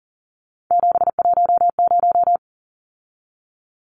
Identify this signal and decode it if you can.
Morse code, heard as a keyed tone, about 40 wpm, 700 Hz. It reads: 710
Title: Numbers - 3-Digit 40wpm